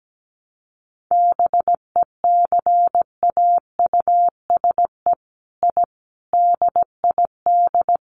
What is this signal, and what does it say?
Morse code, heard as a keyed tone, about 17 wpm, 700 Hz. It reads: BECAUSE I DID
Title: Sets of 5 Words Encore2 17wpm